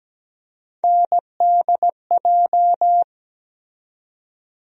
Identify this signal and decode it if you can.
Morse code, heard as a keyed tone, about 17 wpm, 700 Hz. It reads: NDJ
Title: USA Call Sign Postfixes - 3 Letters 17wpm